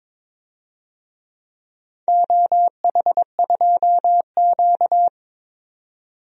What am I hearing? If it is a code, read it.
Morse code, heard as a keyed tone, about 23 wpm, 700 Hz. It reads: OH2Q
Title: DXCC-100 Call Signs - 4 Characters - 22wpm